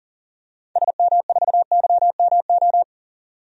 Morse code, heard as a keyed tone, about 40 wpm, 700 Hz. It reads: SM4YMO
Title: DXCC-100 Call Signs - 6 Characters - 40wpm